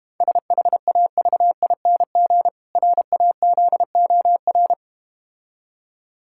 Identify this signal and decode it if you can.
Morse code, heard as a keyed tone, about 32 wpm, 700 Hz. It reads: SHAVING RAZOR